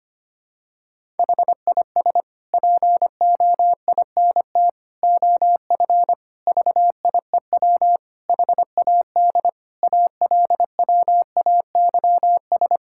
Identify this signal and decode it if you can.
Morse code, heard as a keyed tone, about 25 wpm, 700 Hz. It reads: HIS POINT OF VIEW HAD ALWAYS
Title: Sentences from Top 1000 Words - Review Part 6 - 25wpm